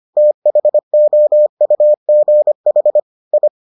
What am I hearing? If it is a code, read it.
Morse code, heard as a keyed tone, about 25 wpm, 600 Hz. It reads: THOUGH I